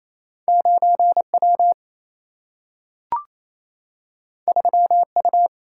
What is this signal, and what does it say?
Morse code, heard as a keyed tone, about 28 wpm, 700 Hz. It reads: 9 W 3 U
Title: Single-Letter Single-Number 28wpm